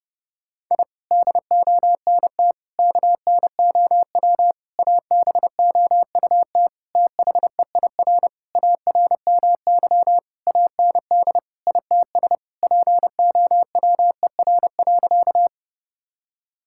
Morse code, heard as a keyed tone, about 30 wpm, 700 Hz. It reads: I DONT KNOW ABOUT THEIR ARMY AND ITS POWER.
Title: Sentences from Top 1000 Words - Review Part 5 - 30wpm